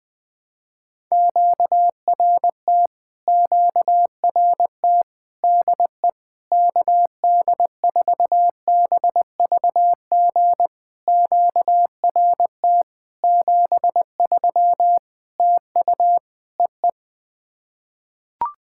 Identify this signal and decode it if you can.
Morse code, heard as a keyed tone, about 20 wpm, 700 Hz. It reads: QRT QRT DE KD4BVG QRT 73 TU EE
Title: POTA QSO Practice - 20wpm